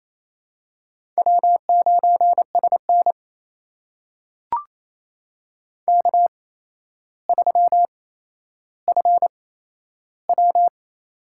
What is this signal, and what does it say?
Morse code, heard as a keyed tone, about 28 wpm, spaced as for 12 wpm, 700 Hz. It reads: W9SN K3FW
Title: USA Call Signs - 4 Characters - 8x Character Spacing - 28wpm